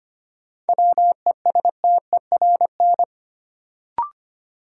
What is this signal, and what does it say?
Morse code, heard as a keyed tone, about 25 wpm, 700 Hz. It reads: WESTERN
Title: Sentences from Top 1000 Words - Review Part 7 - New Word Drill - 25wpm